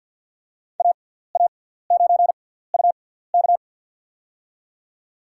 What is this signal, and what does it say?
Morse code, heard as a keyed tone, about 52 wpm, spaced as for 17 wpm, 700 Hz. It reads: AA9UK
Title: USA Call Signs - 5 Characters - 6x Character Spacing - 50wpm